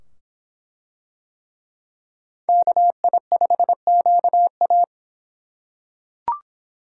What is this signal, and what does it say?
Morse code, heard as a keyed tone, about 23 wpm, 700 Hz. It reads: KI5QA